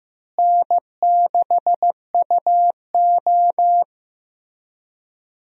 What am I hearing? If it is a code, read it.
Morse code, heard as a keyed tone, about 15 wpm, 700 Hz. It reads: N6UO
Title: USA Call Signs - 4 Characters - Repeated 3x - 15wpm